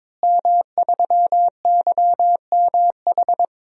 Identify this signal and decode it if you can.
Morse code, heard as a keyed tone, about 21 wpm, 700 Hz. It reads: M3YMH
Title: DXCC-100 Call Signs - 5 Characters - Repeated 2x - 22wpm